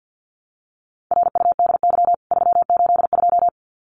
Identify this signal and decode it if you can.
Morse code, heard as a keyed tone, about 50 wpm, 700 Hz. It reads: RUDY 382
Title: CWT Exchange 50wpm